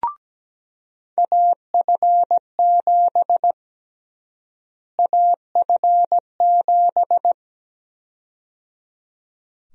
Morse code, heard as a keyed tone, about 17 wpm, 700 Hz. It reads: AF7 AF7
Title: USA Call Sign Prefixes - Repeated 2x - 17wpm